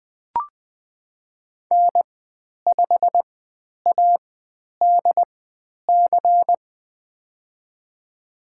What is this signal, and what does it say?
Morse code, heard as a keyed tone, about 20 wpm, spaced as for 10 wpm, 700 Hz. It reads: N5ADC
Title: 095 Lesson 16g - Letter c - Call Signs 20wpm